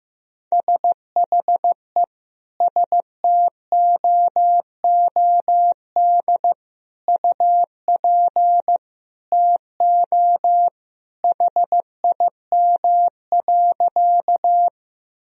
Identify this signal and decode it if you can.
Morse code, heard as a keyed tone, about 15 wpm, 700 Hz. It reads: SHE STOOD UP TO HIM.
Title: Sentences from Top 1000 Words - Review Part 2 - 15wpm